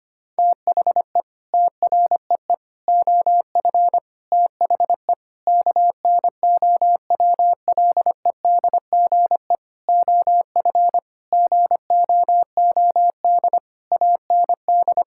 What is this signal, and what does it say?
Morse code, heard as a keyed tone, about 25 wpm, 700 Hz. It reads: THE TREE OF THE KNOWLEDGE OF GOOD AND
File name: Sets of 9 Words - 25wpm